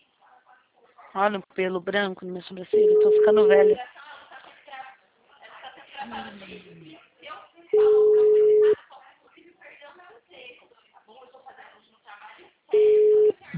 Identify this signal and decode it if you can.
Ringback tone (Europe, 425 Hz), with speech.